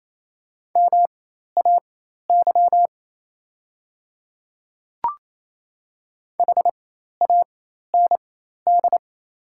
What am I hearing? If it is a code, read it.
Morse code, heard as a keyed tone, about 28 wpm, spaced as for 13 wpm, 700 Hz. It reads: MAY HAND
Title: 4x Character Spacing - Top 200 Words - 28wpm